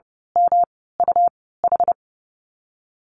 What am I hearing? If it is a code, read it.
Morse code, heard as a keyed tone, about 30 wpm, 700 Hz. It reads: M U H
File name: Lesson 19g - Letter w - Call Signs 30wpm